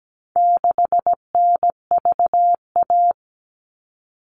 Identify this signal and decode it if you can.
Morse code, heard as a keyed tone, about 17 wpm, 700 Hz. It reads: 6NVA
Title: All Cut Numbers - 4-Digit 17wpm